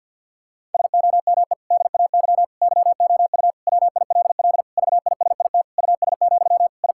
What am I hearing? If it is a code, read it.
Morse code, heard as a keyed tone, about 50 wpm, 700 Hz. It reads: SOME DAY YOU WILL VISIT US, I
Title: Sentences from Top 1000 Words - Review Part 6 - 50wpm